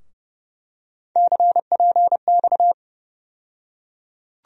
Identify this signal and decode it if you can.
Morse code, heard as a keyed tone, about 30 wpm, 700 Hz. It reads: CPX
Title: USA Call Sign Postfixes - 3 Letters 30wpm